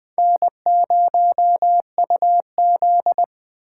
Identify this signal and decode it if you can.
Morse code, heard as a keyed tone, about 24 wpm, 700 Hz. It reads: N0UZ
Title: Speed Racing - USA Call Signs 20wpm